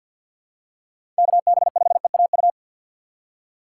Morse code, heard as a keyed tone, about 46 wpm, 700 Hz. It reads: KB5EAU
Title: USA Call Signs - 3x Character Spacing - 50wpm